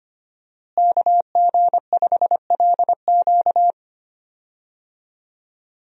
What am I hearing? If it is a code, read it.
Morse code, heard as a keyed tone, about 25 wpm, 700 Hz. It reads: KG5LQ
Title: USA Call Signs - 5 Characters - Repeated 3x - 25wpm